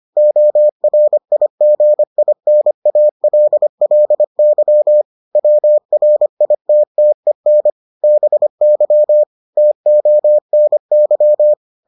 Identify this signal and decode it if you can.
Morse code, heard as a keyed tone, about 25 wpm, 600 Hz. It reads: ORIGINALLY WRITTEN BY TONY